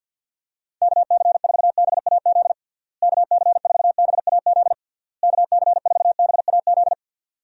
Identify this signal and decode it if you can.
Morse code, heard as a keyed tone, about 51 wpm, 700 Hz. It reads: KC4BAZ KC4BAZ KC4BAZ
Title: USA Call Signs - 6 Characters - Repeated 3x - 50wpm